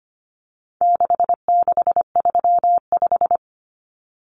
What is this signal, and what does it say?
Morse code, heard as a keyed tone, about 25 wpm, 700 Hz. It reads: 6635
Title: Numbers - 4-Digit 25wpm